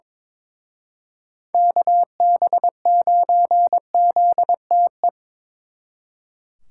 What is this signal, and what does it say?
Morse code, heard as a keyed tone, about 25 wpm, 700 Hz. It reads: KB9ZTE
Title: Speed Racing - USA Call Signs 22wpm